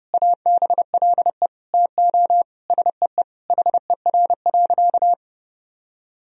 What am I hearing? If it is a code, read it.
Morse code, heard as a keyed tone, about 30 wpm, 700 Hz. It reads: ABLE TO SEE HER.
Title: Sentences from Top 1000 Words - Review Part 2 - 30wpm